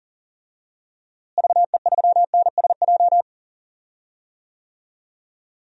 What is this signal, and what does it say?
Morse code, heard as a keyed tone, about 40 wpm, 700 Hz. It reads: VE3NSJ
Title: DXCC-100 Call Signs - 6 Characters - Repeated 3x - 40wpm